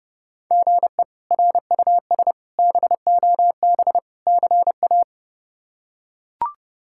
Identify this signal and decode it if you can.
Morse code, heard as a keyed tone, about 30 wpm, 700 Hz. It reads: GE RUS BOB CA
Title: K1USN SST QSO Practice - 30wpm